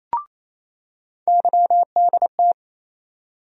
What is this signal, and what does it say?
Morse code, heard as a keyed tone, about 28 wpm, 700 Hz. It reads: YDT
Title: USA Call Sign Postfixes - 3 Letters 28wpm